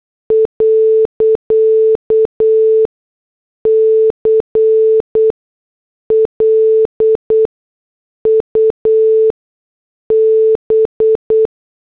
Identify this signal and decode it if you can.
Morse code, heard as a keyed tone, about 8 wpm, spaced as for 6 wpm, 440 Hz. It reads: .CLUB
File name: morse / headlines